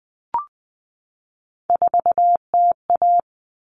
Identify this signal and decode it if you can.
Morse code, heard as a keyed tone, about 20 wpm, 700 Hz. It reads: 4TA
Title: Common Cut Numbers - 3-Digit 20wpm